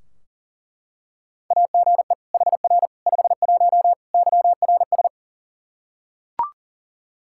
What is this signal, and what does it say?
Morse code, heard as a keyed tone, about 40 wpm, 700 Hz. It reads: AGE HR 51 YRS
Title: Practice QSO - Exchange 3 - More Info - 40wpm